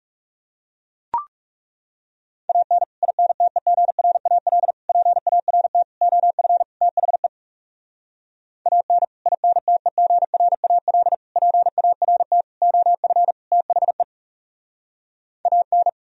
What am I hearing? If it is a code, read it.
Morse code, heard as a keyed tone, about 40 wpm, 700 Hz. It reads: AN INTEGRAL PART OF THE AN INTEGRAL PART OF THE AN